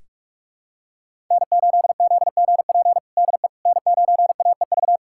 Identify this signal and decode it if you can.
Morse code, heard as a keyed tone, about 41 wpm, 700 Hz. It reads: N8ZGP DE N9AEV
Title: USA Callsign DE Callsign - 5 Characters - 45wpm